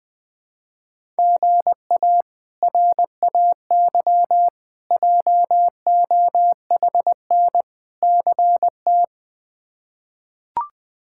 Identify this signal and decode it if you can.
Morse code, heard as a keyed tone, about 20 wpm, 700 Hz. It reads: GA RAY JOHN CT
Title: K1USN SST QSO Practice - 20wpm